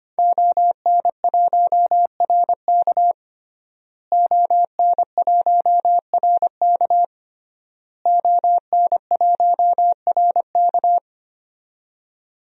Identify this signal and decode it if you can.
Morse code, heard as a keyed tone, about 25 wpm, 700 Hz. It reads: ON1RK ON1RK ON1RK
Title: DXCC-100 Call Signs - 5 Characters - Repeated 3x - 25wpm